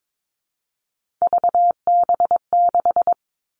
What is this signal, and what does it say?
Morse code, heard as a keyed tone, about 22 wpm, 700 Hz. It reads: VB6
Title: Three Challenging Letter-Number Combinations 22wpm